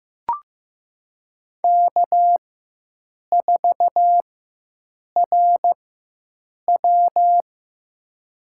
Morse code, heard as a keyed tone, about 15 wpm, spaced as for 7 wpm, 700 Hz. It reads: K4RW
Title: USA Call Signs - 4x Character Spacing - 15wpm